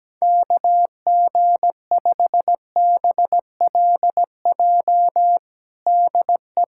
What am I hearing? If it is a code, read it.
Morse code, heard as a keyed tone, about 17 wpm, 700 Hz. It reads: KG5BLJ DE
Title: USA Callsign DE Callsign 17wpm